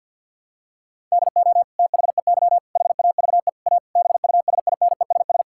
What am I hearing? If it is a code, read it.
Morse code, heard as a keyed tone, about 53 wpm, 700 Hz. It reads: DO THEY HAVE A BUSINESS
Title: Sentences from Top 1000 Words - Review Part 5 - 50wpm